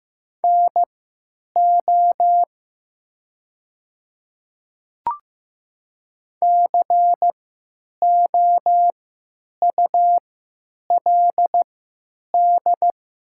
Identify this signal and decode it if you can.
Morse code, heard as a keyed tone, about 15 wpm, spaced as for 8 wpm, 700 Hz. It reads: NO COULD
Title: 3x Character Spacing - Top 100 Words - 15wpm